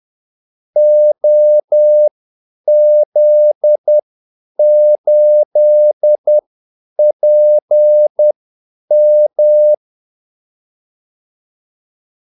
Morse code, heard as a keyed tone, about 10 wpm, spaced as for 8 wpm, 600 Hz. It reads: OZ8PM
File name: Callsigns-10-8wpm